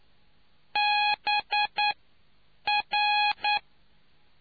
Morse code, heard as a keyed tone, about 10 wpm, spaced as for 7 wpm, 830 Hz. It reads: BR